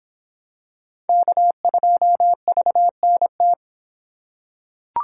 Morse code, heard as a keyed tone, about 24 wpm, 700 Hz. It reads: K2VNT